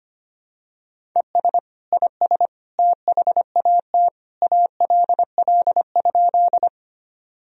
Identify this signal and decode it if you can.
Morse code, heard as a keyed tone, about 25 wpm, 700 Hz. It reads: ES IS THAT ALL?
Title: Sentences from Top 100 Words and Abbreviations 25wpm